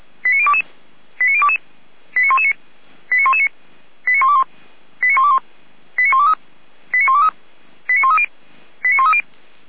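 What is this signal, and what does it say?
Five-tone selective call (ZVEI), heard as quick runs of five tones, tones of about 70 ms. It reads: 89022 89022 89119 89119 89121 89121 89123 89123 89130 89130
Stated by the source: ZVEI five-tone selective call 89017, received on an SDR.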